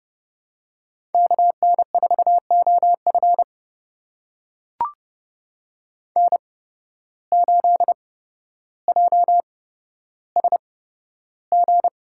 Morse code, heard as a keyed tone, about 30 wpm, spaced as for 13 wpm, 700 Hz. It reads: KN4OF N8JSG
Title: USA Call Signs - 5 Characters - 8x Character Spacing - 30wpm